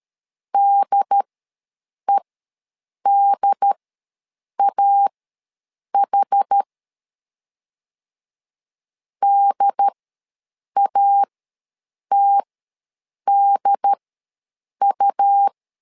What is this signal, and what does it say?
Morse code, heard as a keyed tone, about 13 wpm, spaced as for 7 wpm, 800 Hz. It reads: DEDAH DATDU
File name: LEZIONI / 2 UHDN / 2 UHD ws 12wpm 1722